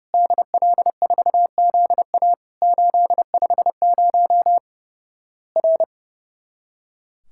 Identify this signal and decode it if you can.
Morse code, heard as a keyed tone, about 30 wpm, 700 Hz. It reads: DL4ZA 850 R
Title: Worked All Europe DX Contest - QTC Exchange - 30wpm